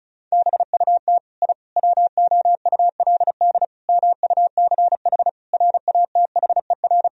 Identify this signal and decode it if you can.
Morse code, heard as a keyed tone, about 36 wpm, 700 Hz. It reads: BUT I WOULD MUCH RATHER
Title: Sentences from Top 1000 Words - Review Part 4 - 35wpm